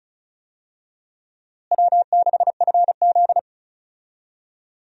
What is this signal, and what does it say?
Morse code, heard as a keyed tone, about 34 wpm, 700 Hz. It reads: W6FZ
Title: USA Call Signs - 4 Characters - Repeated 2x - 35wpm